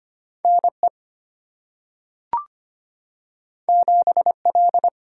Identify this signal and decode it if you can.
Morse code, heard as a keyed tone, about 25 wpm, 700 Hz. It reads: N E 7 L
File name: Two Letter-Number Combinations - 25wpm